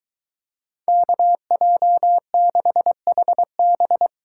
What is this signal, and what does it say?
Morse code, heard as a keyed tone, about 23 wpm, 700 Hz. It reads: KJ6HB